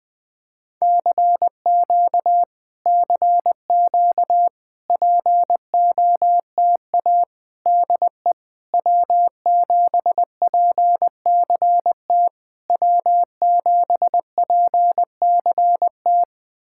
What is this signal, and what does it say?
Morse code, heard as a keyed tone, about 20 wpm, 700 Hz. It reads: CQ CQ POTA DE W7PCT W7PCT
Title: POTA QSO Practice - 20wpm